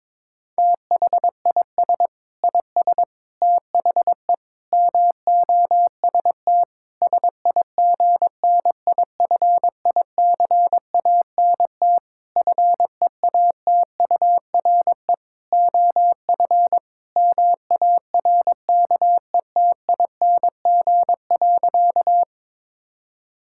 Morse code, heard as a keyed tone, about 22 wpm, 700 Hz. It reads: THIS IS THE MOST SIGNIFICANT FEATURE OF MARKETING.
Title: Sentences from Top 1000 Words - Review Part 6 - 22wpm